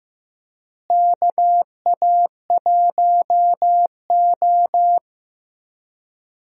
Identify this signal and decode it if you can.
Morse code, heard as a keyed tone, about 15 wpm, 700 Hz. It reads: KA1O
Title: USA Call Signs - 4 Characters - Repeated 3x - 15wpm